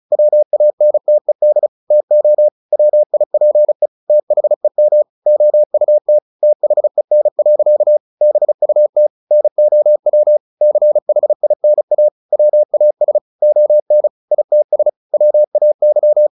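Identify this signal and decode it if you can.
Morse code, heard as a keyed tone, about 34 wpm, 600 Hz. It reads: WANTED TO WIPE THEM OUT THEN. BUT NOW CHINA WAS ON ITS WAY
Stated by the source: Morse practice at 35 wpm from CW med Gnister, sending Ken Follett's novel Never in English.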